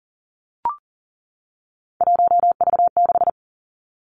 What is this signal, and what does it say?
Morse code, heard as a keyed tone, about 38 wpm, 700 Hz. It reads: 1V6
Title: Three Challenging Letter-Number Combinations 40wpm